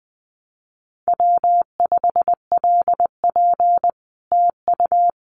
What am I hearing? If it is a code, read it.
Morse code, heard as a keyed tone, about 20 wpm, 700 Hz. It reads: W5LP TU
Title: Common QSO Phrases Encore 20wpm